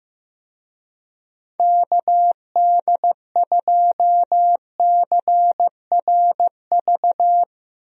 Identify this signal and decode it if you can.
Morse code, heard as a keyed tone, about 15 wpm, 700 Hz. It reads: KD2CRV
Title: USA Call Signs - 6 Characters - Repeated 2x - 15wpm